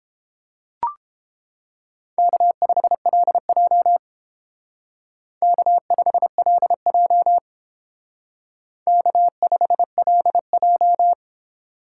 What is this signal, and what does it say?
Morse code, heard as a keyed tone, about 30 wpm, 700 Hz. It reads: K5LJ K5LJ K5LJ